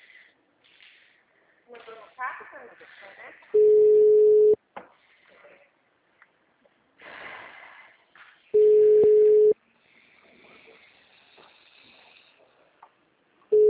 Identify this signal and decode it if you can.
Ringback tone (Europe, 425 Hz), with speech.
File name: out- -2010- - - .19733